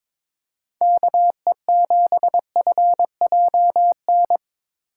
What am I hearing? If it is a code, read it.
Morse code, heard as a keyed tone, about 22 wpm, 700 Hz. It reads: KE7FJN
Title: USA Call Signs - 8x Character Spacing - 22wpm